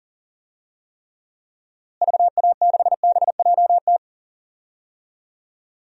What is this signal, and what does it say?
Morse code, heard as a keyed tone, about 38 wpm, 700 Hz. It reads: VA6BJT
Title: DXCC-100 Call Signs - 6 Characters - Repeated 2x - 40wpm